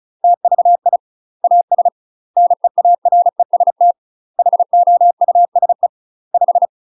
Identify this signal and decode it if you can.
Morse code, heard as a keyed tone, about 36 wpm, 700 Hz. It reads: TVI AS NEAREST HOUSE 5
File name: UK MASTER QSOs TRAINING / 35WPM / UK MASTER QSOs 0018